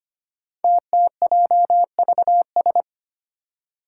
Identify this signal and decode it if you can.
Morse code, heard as a keyed tone, about 25 wpm, 700 Hz. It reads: TTJVS
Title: Five Challenging Letter-Number Combinations 25wpm